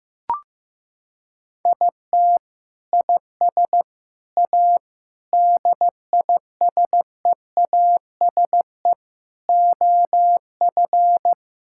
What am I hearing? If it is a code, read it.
Morse code, heard as a keyed tone, about 15 wpm, 700 Hz. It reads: IT IS A DISEASE OF
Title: Sentences from Top 1000 Words - Review Part 6 - 15wpm